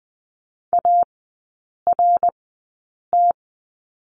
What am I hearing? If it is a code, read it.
Morse code, heard as a keyed tone, about 20 wpm, spaced as for 8 wpm, 700 Hz. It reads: ART
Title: 053 Lesson 10e - Letter r - Three Characters 20wpm